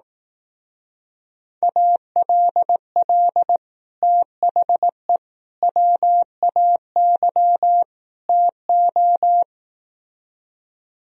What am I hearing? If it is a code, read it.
Morse code, heard as a keyed tone, about 18 wpm, 700 Hz. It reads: ALL THE WAY TO
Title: Speed-Racing - Sets of 4 Words 15wpm